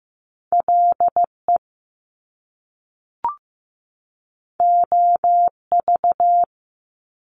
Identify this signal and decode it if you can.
Morse code, heard as a keyed tone, about 15 wpm, 700 Hz. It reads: LE OV
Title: Top Two Letter Combinations in Words 15wpm